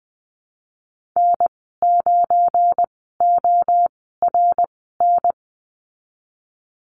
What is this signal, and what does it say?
Morse code, heard as a keyed tone, about 20 wpm, spaced as for 15 wpm, 700 Hz. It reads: N9ORN
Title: 193 Lesson 31g - Letter v - Call Signs 20wpm